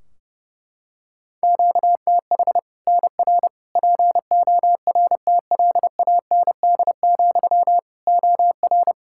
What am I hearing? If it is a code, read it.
Morse code, heard as a keyed tone, about 32 wpm, 700 Hz. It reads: QTH NR PORTLAND, OR
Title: Practice QSO - Exchange 1 - Beginning - 30wpm